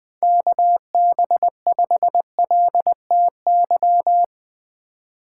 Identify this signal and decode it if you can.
Morse code, heard as a keyed tone, about 20 wpm, 700 Hz. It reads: KB5LTY
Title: USA Call Signs - 6 Characters - 4x Character Spacing - 20wpm